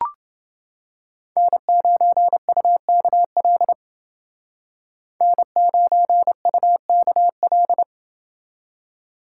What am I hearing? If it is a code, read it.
Morse code, heard as a keyed tone, about 29 wpm, 700 Hz. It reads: N9UKL N9UKL